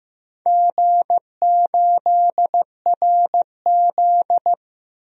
Morse code, heard as a keyed tone, about 15 wpm, 700 Hz. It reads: G8RZ
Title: DXCC-100 Call Signs - 4 Characters - Repeated 3x - 15wpm